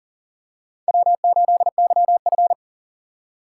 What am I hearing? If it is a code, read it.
Morse code, heard as a keyed tone, about 40 wpm, 700 Hz. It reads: W8YF
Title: USA Call Signs - 4 Characters - 6x Character Spacing - 40wpm